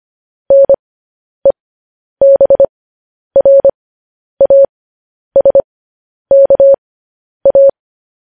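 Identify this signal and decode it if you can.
Morse code, heard as a keyed tone, about 25 wpm, spaced as for 10 wpm, 550 Hz. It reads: NEBRASKA
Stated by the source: Morse code news headlines practice transmission.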